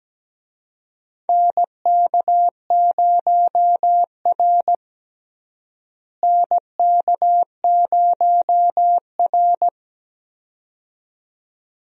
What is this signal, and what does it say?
Morse code, heard as a keyed tone, about 17 wpm, 700 Hz. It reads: NK0R NK0R
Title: USA Call Signs - 4 Characters - Repeated 2x - 17wpm